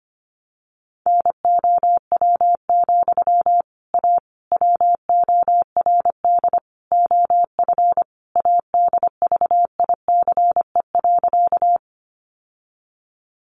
Morse code, heard as a keyed tone, about 25 wpm, 700 Hz. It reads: NOW, A WORD OF ADVICE.
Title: Sentences from Top 1100 Words 25wpm